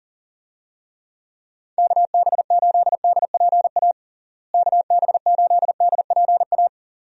Morse code, heard as a keyed tone, about 40 wpm, 700 Hz. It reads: KB8DPA KB8DPA
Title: USA Call Signs - 6 Characters - Repeated 2x - 40wpm